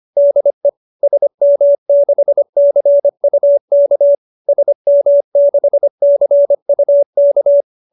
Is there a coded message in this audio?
Yes: Morse code from 0.2 to 7.6 s, heard as a keyed tone, about 25 wpm, 590 Hz. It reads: DE SM6CUK SM6CUK